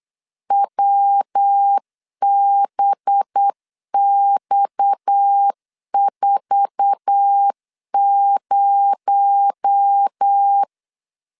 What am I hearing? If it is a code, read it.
Morse code, heard as a keyed tone, about 8 wpm, 800 Hz. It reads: WBX40
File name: LEZIONI / 8 YJ45 / 8 YJ4 8wpm 3246